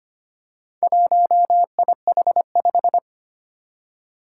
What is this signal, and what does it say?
Morse code, heard as a keyed tone, about 25 wpm, 700 Hz. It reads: 1IH5
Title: Four Challenging Letter-Number Combinations 25wpm